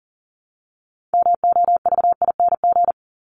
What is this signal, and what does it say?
Morse code, heard as a keyed tone, about 40 wpm, 700 Hz. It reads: MOVING
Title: ING Suffix - Top 100 Words with Base Words 40wpm